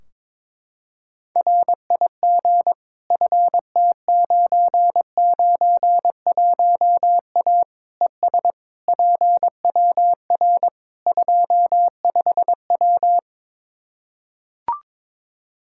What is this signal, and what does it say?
Morse code, heard as a keyed tone, about 22 wpm, 700 Hz. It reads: RIG FT991A ES PWR 25W
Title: Practice QSO - Exchange 2 - Info - 22wpm